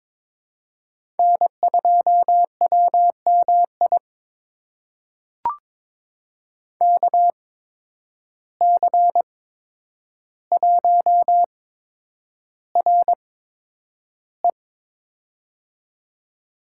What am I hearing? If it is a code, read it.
Morse code, heard as a keyed tone, about 22 wpm, spaced as for 9 wpm, 700 Hz. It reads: N2WMI KC1RE
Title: USA Call Signs - 5 Characters - 8x Character Spacing - 22wpm